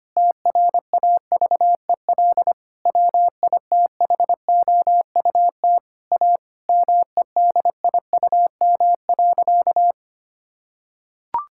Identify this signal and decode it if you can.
Morse code, heard as a keyed tone, about 25 wpm, 700 Hz. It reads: TRAVEL WITHOUT A MEDIUM.
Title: Sentences from Top 1000 Words - Review Part 6 - 25wpm